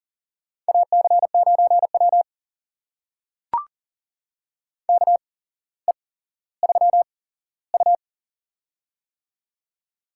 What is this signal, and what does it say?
Morse code, heard as a keyed tone, about 39 wpm, spaced as for 17 wpm, 700 Hz. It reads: AC9W KE3U
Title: USA Call Signs - 4 Characters - 8x Character Spacing - 40wpm